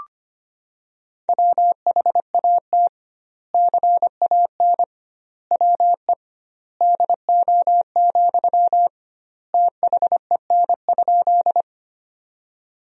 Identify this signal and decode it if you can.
Morse code, heard as a keyed tone, about 25 wpm, 700 Hz. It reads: WHAT CAN WE DO, THEN?
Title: 2x Word-Spacing - Sentences from Top 100 Words - 25wpm